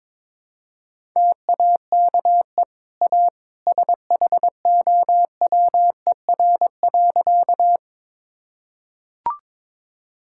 Morse code, heard as a keyed tone, about 22 wpm, 690 Hz. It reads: TAKE A SHOWER.
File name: practice / 22wpm / Easy Sentences with Hints - 22wpm